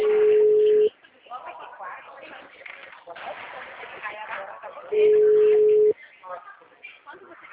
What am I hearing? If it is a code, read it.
Ringback tone (Europe, 425 Hz), with speech.